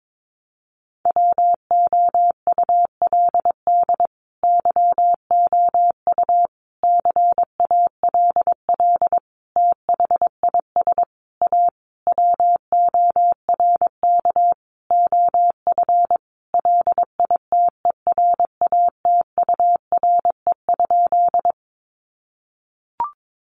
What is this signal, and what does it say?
Morse code, heard as a keyed tone, about 22 wpm, 700 Hz. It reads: WOULD YOU CALL THIS A WORK OF LITERATURE?
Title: Sentences from Top 600 Words 22wpm